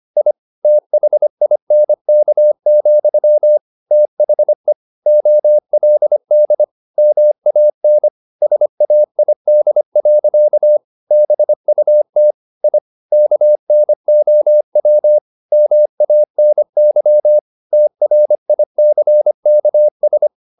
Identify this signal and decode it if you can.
Morse code, heard as a keyed tone, about 25 wpm, 600 Hz. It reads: I THINK, THE OLD MAN SAID. BUT I KNOW MANY TRICKS
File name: the old man and the sea part 1 25wpm